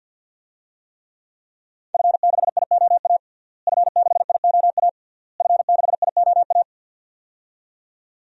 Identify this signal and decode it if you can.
Morse code, heard as a keyed tone, about 53 wpm, 700 Hz. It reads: F6IOA F6IOA F6IOA
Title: DXCC-100 Call Signs - 5 Characters - Repeated 3x - 50wpm